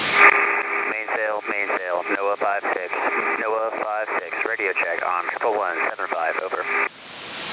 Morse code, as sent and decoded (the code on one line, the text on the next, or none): none